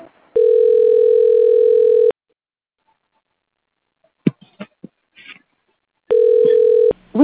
Ringback tone (North America), 440 + 480 Hz.